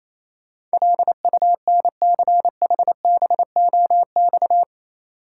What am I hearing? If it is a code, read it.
Morse code, heard as a keyed tone, about 27 wpm, 690 Hz. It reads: LUNCHBOX
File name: Easy Words - Speed-Racing with Hints - 22wpm